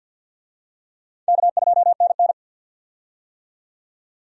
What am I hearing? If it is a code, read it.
Morse code, heard as a keyed tone, about 50 wpm, 700 Hz. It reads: K2NN
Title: USA Call Signs - 4 Characters - 50wpm